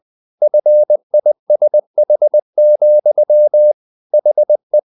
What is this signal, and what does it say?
Morse code, heard as a keyed tone, about 20 wpm, 600 Hz. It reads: FISH, HE
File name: the old man and the sea part 2 20wpm